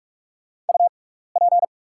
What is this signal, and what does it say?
Morse code, heard as a keyed tone, about 44 wpm, spaced as for 15 wpm, 700 Hz. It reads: UP
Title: 6x Character Spacing - Top 300 Words - 45wpm